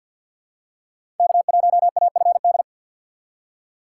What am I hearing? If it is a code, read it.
Morse code, heard as a keyed tone, about 53 wpm, 700 Hz. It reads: K1AFD
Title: USA Call Signs - 5 Characters - 4x Character Spacing - 50wpm